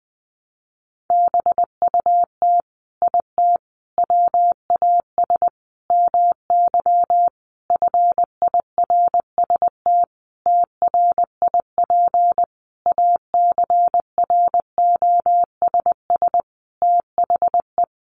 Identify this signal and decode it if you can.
Morse code, heard as a keyed tone, about 20 wpm, 700 Hz. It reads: BUT IT WAS MY FIRST TRIP ACROSS THE
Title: Sentences from Top 1300 Words 20wpm